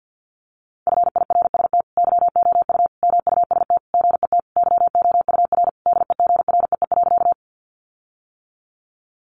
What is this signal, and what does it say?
Morse code, heard as a keyed tone, about 50 wpm, 700 Hz. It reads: FIRST YOU MUST GET YOUR DEGREE.
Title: Sentences from Top 1500 Words 50wpm